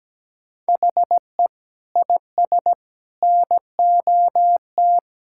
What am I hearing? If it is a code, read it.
Morse code, heard as a keyed tone, about 17 wpm, 700 Hz. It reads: HE IS NOT
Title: Sentences from Top 1600 Words 17wpm